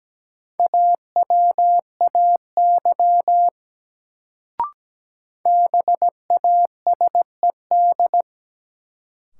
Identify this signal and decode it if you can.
Morse code, heard as a keyed tone, about 17 wpm, 700 Hz. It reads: AWAY BASED
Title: Sets of 2 Words - New Word Drill - 17wpm